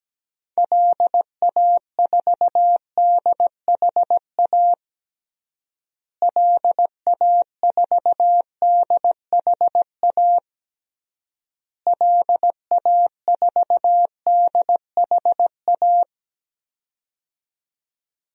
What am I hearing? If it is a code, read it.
Morse code, heard as a keyed tone, about 17 wpm, 700 Hz. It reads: LA4DHA LA4DHA LA4DHA
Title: DXCC-100 Call Signs - 6 Characters - Repeated 3x - 17wpm